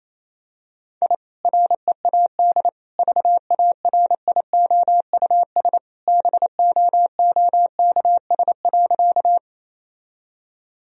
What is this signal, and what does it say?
Morse code, heard as a keyed tone, about 28 wpm, 700 Hz. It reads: I READ VARIOUS BOOKS.
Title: Sentences from Top 1000 Words - Review Part 4 - 28wpm